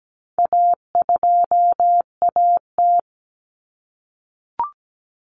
Morse code, heard as a keyed tone, about 17 wpm, 700 Hz. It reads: A2AT
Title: Common Cut Numbers - 4-Digit 17wpm